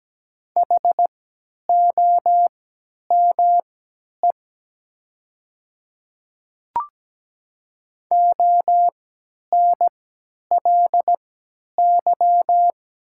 Morse code, heard as a keyed tone, about 17 wpm, spaced as for 10 wpm, 700 Hz. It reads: HOME ONLY
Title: 3x Character Spacing - Top 300 Words - 17wpm